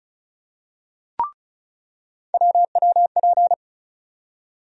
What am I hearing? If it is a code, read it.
Morse code, heard as a keyed tone, about 36 wpm, 700 Hz. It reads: WWP
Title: USA Call Sign Postfixes - 3 Letters 35wpm